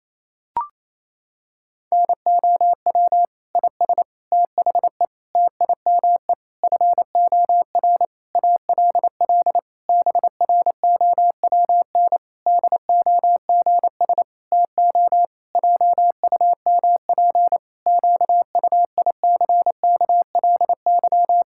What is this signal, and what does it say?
Morse code, heard as a keyed tone, about 28 wpm, 700 Hz. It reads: NOW IS THE TIME FOR ALL BROWN DOGS TO JUMP QUICKLY
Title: Pangram Challenge v2 - 28wpm